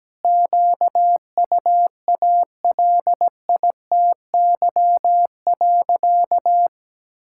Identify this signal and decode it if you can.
Morse code, heard as a keyed tone, about 17 wpm, 700 Hz. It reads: QUALITY.
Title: Sentences from Top 1000 Words - Review Part 5 - 17wpm